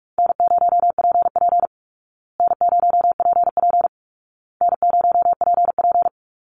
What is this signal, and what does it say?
Morse code, heard as a keyed tone, about 44 wpm, 700 Hz. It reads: N0PP N0PP N0PP
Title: Call Signs Repeated 2x - 45wpm